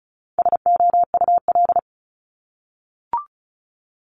Morse code, heard as a keyed tone, about 36 wpm, 700 Hz. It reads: SOUL
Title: Multi-letter Phonemes 35wpm